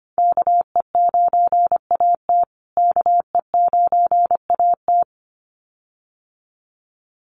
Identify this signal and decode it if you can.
Morse code, heard as a keyed tone, about 25 wpm, 700 Hz. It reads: KE9AT KE9AT
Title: USA Call Signs Repeated 2x 25wpm